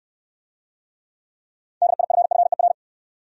Morse code, heard as a keyed tone, about 69 wpm, 700 Hz. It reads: DIFFER